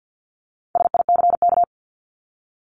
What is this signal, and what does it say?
Morse code, heard as a keyed tone, about 50 wpm, 700 Hz. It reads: SICK
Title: Top 1501-1750 Words 50wpm